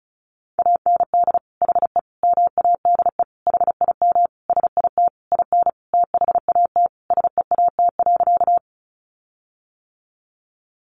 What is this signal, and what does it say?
Morse code, heard as a keyed tone, about 34 wpm, 700 Hz. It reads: AND HE MADE HIM SIT IN THAT SEAT.
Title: Sentences from Top 1500 Words 35wpm